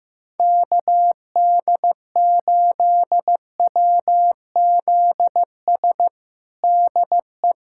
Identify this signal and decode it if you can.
Morse code, heard as a keyed tone, about 15 wpm, 700 Hz. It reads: KD8WZS DE
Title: USA Callsign DE Callsign - 6 Characters - 15wpm